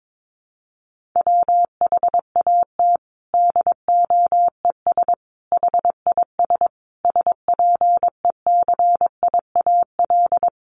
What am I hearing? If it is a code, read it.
Morse code, heard as a keyed tone, about 22 wpm, 700 Hz. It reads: WHAT DOES HIS SPECIAL
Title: Sentences from Top 1400 Words 22wpm